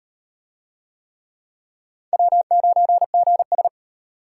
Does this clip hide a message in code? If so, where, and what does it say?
Yes: Morse code from 2.1 to 3.7 s, heard as a keyed tone, about 38 wpm, 700 Hz. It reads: W9GS